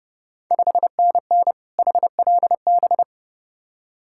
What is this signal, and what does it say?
Morse code, heard as a keyed tone, about 30 wpm, 700 Hz. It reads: 5NN HLB
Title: Ontario QSO Party Exchange 30wpm